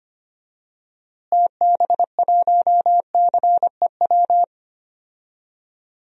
Morse code, heard as a keyed tone, about 25 wpm, 700 Hz. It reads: TB1CEW
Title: DXCC-100 Call Signs - 6 Characters - Repeated 2x - 25wpm